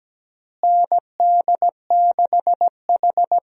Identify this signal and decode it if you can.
Morse code, heard as a keyed tone, about 17 wpm, 700 Hz. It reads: ND6H
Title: USA Call Signs - 4 Characters - Repeated 2x - 17wpm